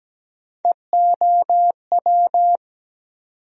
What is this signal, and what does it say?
Morse code, heard as a keyed tone, about 17 wpm, 700 Hz. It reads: EOW
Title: USA Call Sign Postfixes - 3 Letters 17wpm